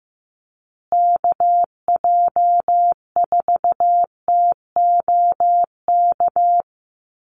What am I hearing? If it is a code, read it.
Morse code, heard as a keyed tone, about 15 wpm, 700 Hz. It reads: KJ4TOK
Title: USA Call Signs Repeated 3x 15wpm